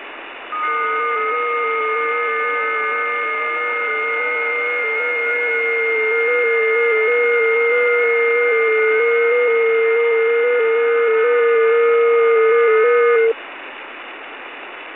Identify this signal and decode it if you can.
FT8.
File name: test / websdr test20